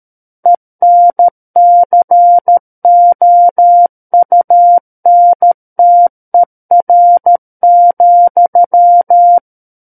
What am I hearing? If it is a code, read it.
Morse code, heard as a keyed tone, about 13 wpm, 700 Hz. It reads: ENCOUNTER,
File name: Book One / 04-Chapter 4 - 13 WPM